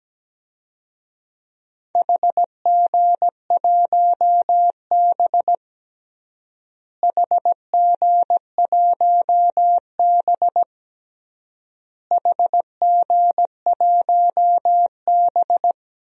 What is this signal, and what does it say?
Morse code, heard as a keyed tone, about 17 wpm, 700 Hz. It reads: HG1B HG1B HG1B
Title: DXCC-100 Call Signs - 4 Characters - Repeated 3x - 17wpm